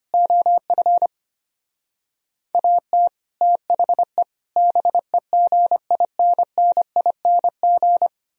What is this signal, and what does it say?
Morse code, heard as a keyed tone, about 33 wpm, 700 Hz. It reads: OF AT THE BEGINNING